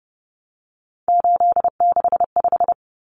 Morse code, heard as a keyed tone, about 30 wpm, 700 Hz. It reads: 865
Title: Common Cut Numbers - 3-Digit 30wpm